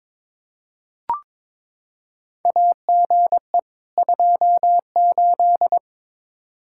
Morse code, heard as a keyed tone, about 22 wpm, 700 Hz. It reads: AGE 28
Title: Practice QSO - Exchange 3 - More Info - 22wpm